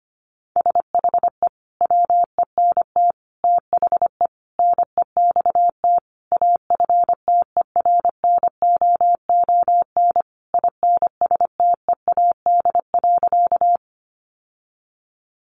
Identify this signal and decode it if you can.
Morse code, heard as a keyed tone, about 25 wpm, 700 Hz. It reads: SHE WENT THE NEXT AFTERNOON INSTEAD.
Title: Sentences from Top 1500 Words 25wpm